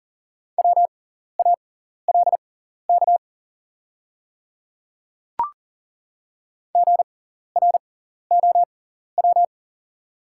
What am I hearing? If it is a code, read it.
Morse code, heard as a keyed tone, about 40 wpm, spaced as for 14 wpm, 700 Hz. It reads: WALK GROW
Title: 6x Character Spacing - Top 300 Words - 40wpm